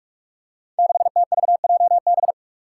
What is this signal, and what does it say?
Morse code, heard as a keyed tone, about 47 wpm, 700 Hz. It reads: 6TVJB
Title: Five Challenging Letter-Number Combinations 45wpm